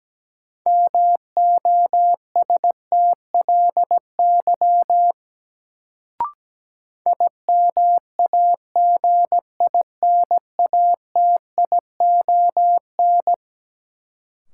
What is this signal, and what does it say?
Morse code, heard as a keyed tone, about 17 wpm, 700 Hz. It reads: MOSTLY IMAGINATION
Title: Sentences from Top 1300 Words - New Word Drill - 17wpm